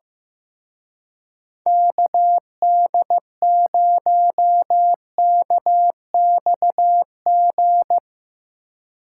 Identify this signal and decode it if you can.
Morse code, heard as a keyed tone, about 15 wpm, 700 Hz. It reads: KD0KXG
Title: Speed Racing - USA Call Signs 15wpm